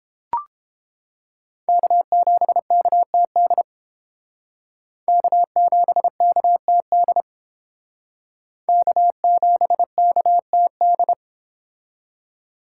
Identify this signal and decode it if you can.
Morse code, heard as a keyed tone, about 28 wpm, 700 Hz. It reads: K7KTD K7KTD K7KTD